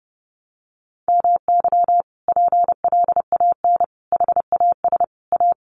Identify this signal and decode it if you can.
Morse code, heard as a keyed tone, about 30 wpm, 700 Hz. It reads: MY PLAN HAS A
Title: Sentences from Top 1300 Words 30wpm